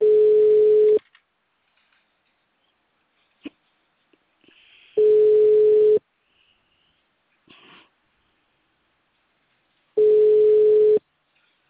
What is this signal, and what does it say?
Ringback tone (Europe, 425 Hz).